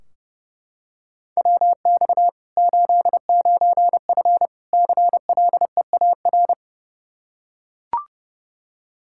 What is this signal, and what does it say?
Morse code, heard as a keyed tone, about 30 wpm, 700 Hz. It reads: WX 89F CLEAR
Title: Practice QSO - Exchange 2 - Info - 30wpm